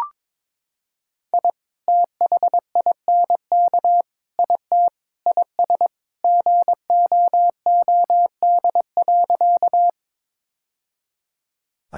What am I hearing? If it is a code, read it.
Morse code, heard as a keyed tone, about 22 wpm, 700 Hz. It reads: I THINK IT IS GOOD.
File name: practice / 22wpm / Sentences with 3-Word Phrases - 22wpm